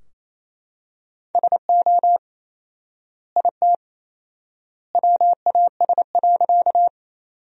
Morse code, heard as a keyed tone, about 28 wpm, 700 Hz. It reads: SO IT WAS.
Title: 4x Word-Spacing - Sentences from Top 100 Words - 28wpm